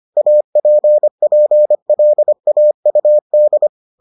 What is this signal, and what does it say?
Morse code, heard as a keyed tone, about 25 wpm, 600 Hz. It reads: APPLAUD